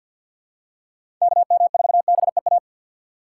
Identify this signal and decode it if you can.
Morse code, heard as a keyed tone, about 47 wpm, 700 Hz. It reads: KM4BEA
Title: USA Call Signs - 6 Characters - 3x Character Spacing - 50wpm